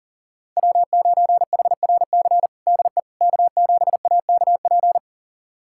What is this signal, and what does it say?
Morse code, heard as a keyed tone, about 40 wpm, 700 Hz. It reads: W9HRC DE K7AKP
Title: USA Callsign DE Callsign - 5 Characters - 40wpm